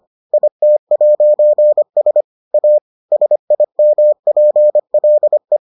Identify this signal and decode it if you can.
Morse code, heard as a keyed tone, about 25 wpm, 600 Hz. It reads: IT'S A SIMPLE